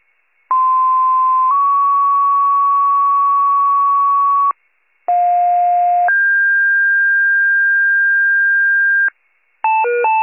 Two-tone page, heard as a one-second tone, then a three-second tone, about 1040 then 1120 Hz, about 710 then 1600 Hz. It is heard from 0.5 to 9.1 s.
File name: pager / 95 RESCUE / wcalert